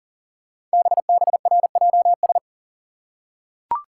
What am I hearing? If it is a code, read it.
Morse code, heard as a keyed tone, about 38 wpm, 690 Hz. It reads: BBRJS
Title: Code Groups - 5 Letters - 40wpm